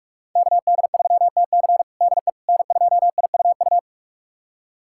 Morse code, heard as a keyed tone, about 42 wpm, 700 Hz. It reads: KD3TC DE N2IUU
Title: USA Callsign DE Callsign - 5 Characters - 45wpm